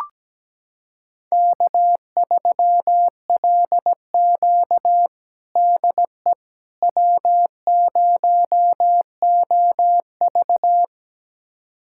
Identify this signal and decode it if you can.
Morse code, heard as a keyed tone, about 17 wpm, 700 Hz. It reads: K3LQ DE W0OV
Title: USA Callsign DE Callsign - 4 Characters - 17wpm